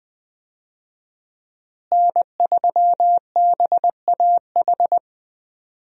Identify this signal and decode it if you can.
Morse code, heard as a keyed tone, about 20 wpm, 700 Hz. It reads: N3BAH
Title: USA Call Signs - 5 Characters - Repeated 2x - 20wpm